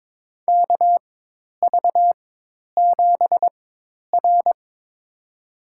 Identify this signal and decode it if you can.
Morse code, heard as a keyed tone, about 22 wpm, spaced as for 10 wpm, 700 Hz. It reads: KV7R
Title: USA Call Signs - 4 Characters - 4x Character Spacing - 22wpm